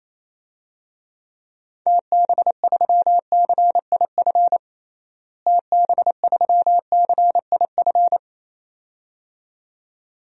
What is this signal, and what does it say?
Morse code, heard as a keyed tone, about 28 wpm, 700 Hz. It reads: TB3CIF TB3CIF
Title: DXCC-100 Call Signs - 6 Characters - Repeated 2x - 28wpm